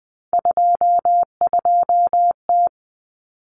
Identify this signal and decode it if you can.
Morse code, heard as a keyed tone, about 20 wpm, 700 Hz. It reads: 22T
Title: Common Cut Numbers - 3-Digit 20wpm